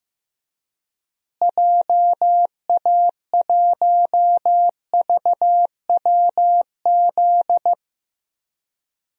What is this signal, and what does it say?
Morse code, heard as a keyed tone, about 16 wpm, 700 Hz. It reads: JA1VWZ
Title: DXCC-100 Call Signs - 6 Characters - Repeated 3x - 15wpm